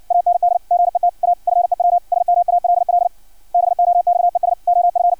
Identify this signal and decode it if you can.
Morse code, heard as a keyed tone, about 56 wpm, 700 Hz. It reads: AND GET A FEW SMALL BOXES OF
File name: archived / CRSnet